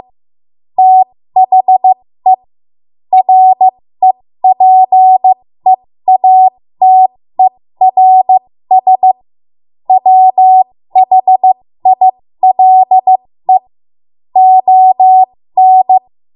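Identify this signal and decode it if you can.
Morse code, heard as a keyed tone, about 15 wpm, spaced as for 13 wpm, 750 Hz. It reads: THE REPEATERS WHILE ON
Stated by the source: ARRL Morse code practice at 13 WPM.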